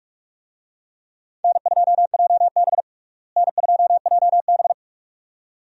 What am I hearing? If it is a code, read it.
Morse code, heard as a keyed tone, about 46 wpm, 700 Hz. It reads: N2JB N2JB
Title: USA Call Signs - 4 Characters - Repeated 2x - 45wpm